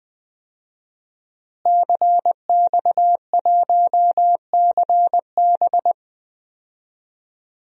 Morse code, heard as a keyed tone, about 20 wpm, 700 Hz. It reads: CX1CB
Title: DXCC-100 Call Signs - 5 Characters - Repeated 3x - 20wpm